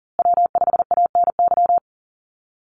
Morse code, heard as a keyed tone, about 40 wpm, 700 Hz. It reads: W5ANY
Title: USA Call Signs Repeated 3x 40wpm